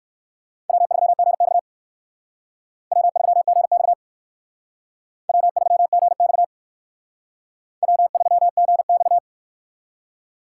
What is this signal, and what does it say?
Morse code, heard as a keyed tone, about 58 wpm, 700 Hz. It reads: W3GX W3GX W3GX W3GX